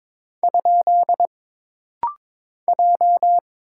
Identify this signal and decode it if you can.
Morse code, heard as a keyed tone, about 22 wpm, 700 Hz. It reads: ? J
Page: This rapid-fire timing is geared to push your ICR (Instant Character Recognition) skills.